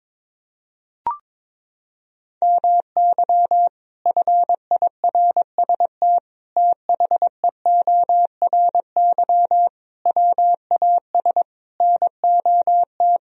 Morse code, heard as a keyed tone, about 22 wpm, 700 Hz. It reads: MY FIRST THEORY WAS NOT
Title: Sentences from Top 1000 Words - Review Part 3 - 22wpm